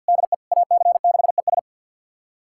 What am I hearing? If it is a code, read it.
Morse code, heard as a keyed tone, about 46 wpm, 700 Hz. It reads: DE AC6ES
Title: USA Callsign DE Callsign - 5 Characters - 50wpm